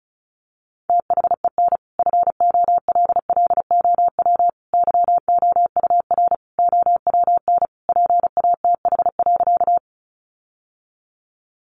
Morse code, heard as a keyed tone, about 34 wpm, 700 Hz. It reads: THEN FOLLOW YOUR OWN PATH.
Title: Sentences from Top 1400 Words 35wpm